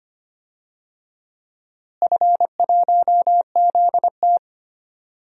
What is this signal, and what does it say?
Morse code, heard as a keyed tone, about 25 wpm, 700 Hz. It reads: F1ZT
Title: DXCC-100 Call Signs - 4 Characters - Repeated 2x - 25wpm